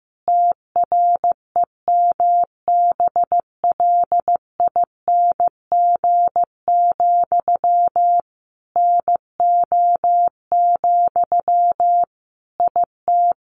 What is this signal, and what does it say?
Morse code, heard as a keyed tone, about 14 wpm, 700 Hz. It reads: TREMBLING, NO, IT
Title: A Sound Of Thunder 15wpm